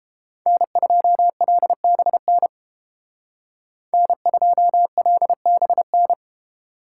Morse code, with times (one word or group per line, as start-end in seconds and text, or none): 0.5-2.5 N2LBN
3.9-6.1 N2LBN